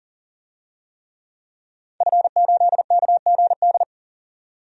Morse code, heard as a keyed tone, about 37 wpm, 700 Hz. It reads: F8KGD
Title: DXCC-100 Call Signs - 5 Characters - 40wpm